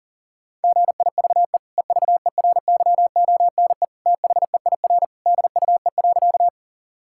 Morse code, heard as a keyed tone, about 40 wpm, 700 Hz. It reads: GIVE EVERYONE THEIR DUE.
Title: Sentences from Top 1000 Words - Review Part 6 - 40wpm